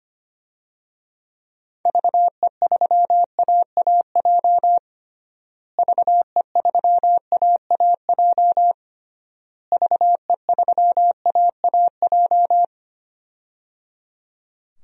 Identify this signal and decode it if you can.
Morse code, heard as a keyed tone, about 25 wpm, 700 Hz. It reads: VE3AAJ VE3AAJ VE3AAJ
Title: DXCC-100 Call Signs - 6 Characters - Repeated 3x - 25wpm